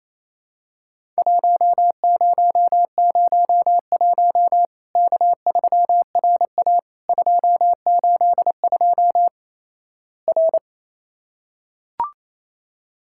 Morse code, heard as a keyed tone, about 28 wpm, 700 Hz. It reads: 1001 K3RA 282 R
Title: Worked All Europe DX Contest - QTC Exchange - 28wpm